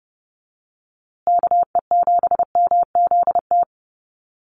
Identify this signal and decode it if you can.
Morse code, heard as a keyed tone, about 30 wpm, 700 Hz. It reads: KE7MZT
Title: USA Call Signs Repeated 2x 30wpm